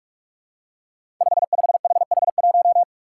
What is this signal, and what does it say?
Morse code, heard as a keyed tone, about 44 wpm, 700 Hz. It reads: 55HH1
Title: Five Challenging Letter-Number Combinations 45wpm